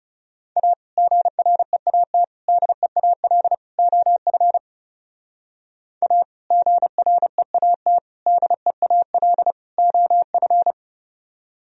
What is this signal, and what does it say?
Morse code, heard as a keyed tone, about 32 wpm, 700 Hz. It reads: A GREAT DEAL OF A GREAT DEAL OF
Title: Speed-Racing - Sets of 4 Words 30wpm